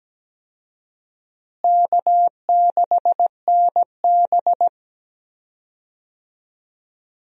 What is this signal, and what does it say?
Morse code, heard as a keyed tone, about 17 wpm, 700 Hz. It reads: K6NB
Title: USA Call Signs - 4 Characters - Repeated 2x - 17wpm